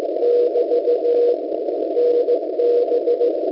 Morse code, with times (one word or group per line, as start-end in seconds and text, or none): 0.2-1.3 <BT>
2.0-3.3 NB